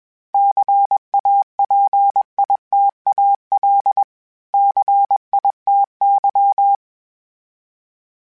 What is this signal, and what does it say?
Morse code, heard as a keyed tone, about 21 wpm, 800 Hz. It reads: CAPITAL CITY
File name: fromSMs / FromSMs mp3s / Cities@20